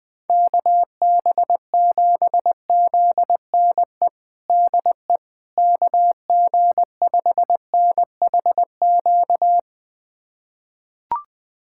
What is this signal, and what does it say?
Morse code, heard as a keyed tone, about 20 wpm, 700 Hz. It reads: KB7ZNE DE KG5NHQ
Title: USA Callsign DE Callsign 20wpm